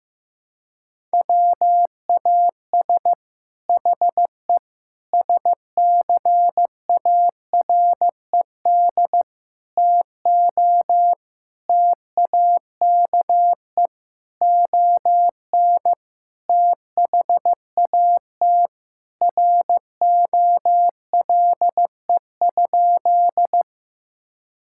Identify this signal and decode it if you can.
Morse code, heard as a keyed tone, about 15 wpm, 700 Hz. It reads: WAS HE SCARED TO TAKE ON THAT ROLE?
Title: Sentences from Top 1000 Words - Review Part 6 - 15wpm